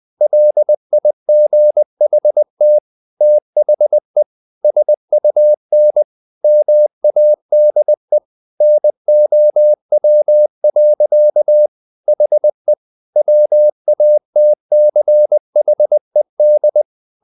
Morse code, heard as a keyed tone, about 20 wpm, 600 Hz. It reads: LIGHT THE SUN MADE NOW. HE WATCHED
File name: the old man and the sea / the old man and the sea part 1 20wpm